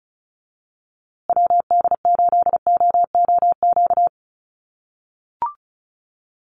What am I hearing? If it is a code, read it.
Morse code, heard as a keyed tone, about 34 wpm, 700 Hz. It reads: WD8OOQ
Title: USA Call Signs Repeated 2x 35wpm